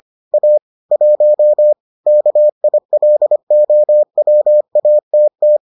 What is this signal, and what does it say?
Morse code, heard as a keyed tone, about 25 wpm, 600 Hz. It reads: A 1 KILOWATT